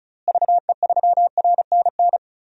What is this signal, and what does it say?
Morse code, heard as a keyed tone, about 34 wpm, 700 Hz. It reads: VE3RNN
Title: DXCC-100 Call Signs - 6 Characters - Repeated 2x - 35wpm